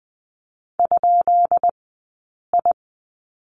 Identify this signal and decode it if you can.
Morse code, heard as a keyed tone, about 20 wpm, 700 Hz. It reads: ? I
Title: 171 Lesson 28d - Number 9 - Two Characters 20wpm